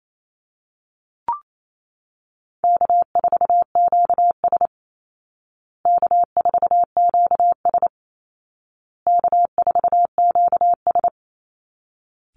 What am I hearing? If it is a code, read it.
Morse code, heard as a keyed tone, about 28 wpm, 700 Hz. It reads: K4QS K4QS K4QS
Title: Call Signs Repeated 2x - 28wpm